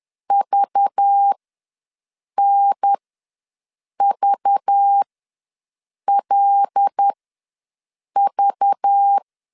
Morse code, heard as a keyed tone, about 10 wpm, spaced as for 6 wpm, 800 Hz. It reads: VNVLV
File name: LEZIONI / 5 LBVC / 5 LBV ws 10wpm 3645